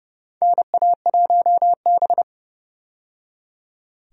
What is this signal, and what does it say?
Morse code, heard as a keyed tone, about 32 wpm, 700 Hz. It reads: NA1B
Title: USA Call Signs - 4 Characters - 30wpm